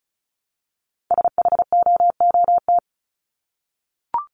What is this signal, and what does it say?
Morse code, heard as a keyed tone, about 34 wpm, 700 Hz. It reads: SHOOT
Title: Top 1001-1250 Words 35wpm